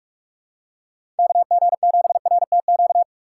Morse code, heard as a keyed tone, about 46 wpm, 700 Hz. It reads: KG7RTQ
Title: USA Call Signs - 6 Characters - 3x Character Spacing - 45wpm